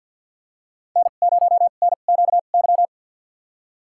Morse code, heard as a keyed tone, about 46 wpm, spaced as for 35 wpm, 700 Hz. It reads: N0NQY
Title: USA Call Signs - 2x Character Spacing - 50wpm